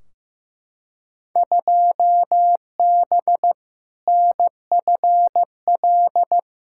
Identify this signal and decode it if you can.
Morse code, heard as a keyed tone, about 15 wpm, 700 Hz. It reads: 2B NFL
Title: Field Day Exchanges Encore - 15wpm